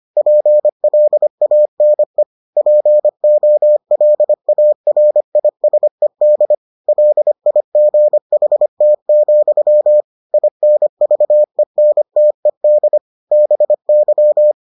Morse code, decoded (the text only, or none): PLANE POLARISED LIGHT, INVENTED BY